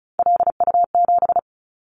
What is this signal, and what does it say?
Morse code, heard as a keyed tone, about 36 wpm, 700 Hz. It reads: LU7
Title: Callsign Prefixes for Top 50 DXCC - 35wpm